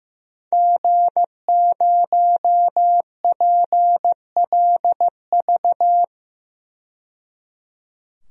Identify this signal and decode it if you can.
Morse code, heard as a keyed tone, about 15 wpm, 700 Hz. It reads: G0PLV
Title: DXCC-100 Call Signs - 5 Characters - 15wpm